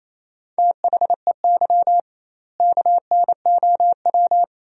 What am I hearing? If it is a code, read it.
Morse code, heard as a keyed tone, about 28 wpm, 700 Hz. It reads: THEY KNOW
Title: 2x Word-Spacing - Sentences from Top 500 Words - 28wpm